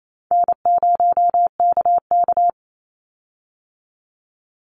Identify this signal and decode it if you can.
Morse code, heard as a keyed tone, about 28 wpm, 700 Hz. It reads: N0KK
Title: Easy Call Signs - 28wpm